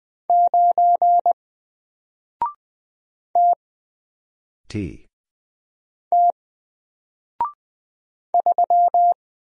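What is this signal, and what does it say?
Morse code, heard as a keyed tone, about 20 wpm, 700 Hz. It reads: 9 T T 3
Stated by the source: Single letter and number Morse drill at 20 wpm, from N4LCD.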